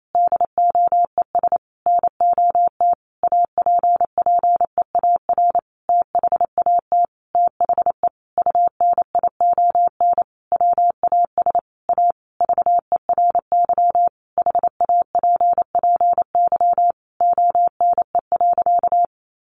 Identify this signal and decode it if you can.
Morse code, heard as a keyed tone, about 28 wpm, 700 Hz. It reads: DOES NOT APPEAR THAT THE UNION WAS A VERY HAPPY ONE.
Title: Sentences from Top 1400 Words 28wpm